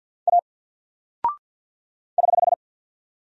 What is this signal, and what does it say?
Morse code, heard as a keyed tone, about 50 wpm, 700 Hz. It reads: A <HH>
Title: Single Character (Letter:Number:Punctuation:Prosign) 50wpm